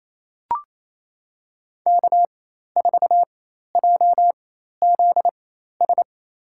Morse code, heard as a keyed tone, about 28 wpm, spaced as for 13 wpm, 700 Hz. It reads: K4JZS
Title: USA Call Signs - 5 Characters - 4x Character Spacing - 28wpm